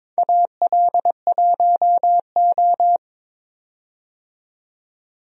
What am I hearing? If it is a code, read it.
Morse code, heard as a keyed tone, about 22 wpm, 700 Hz. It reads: AL1O
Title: USA Call Signs - 4 Characters - 22wpm